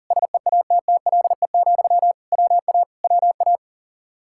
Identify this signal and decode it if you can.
Morse code, heard as a keyed tone, about 52 wpm, 700 Hz. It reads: SEATTLE, WA WA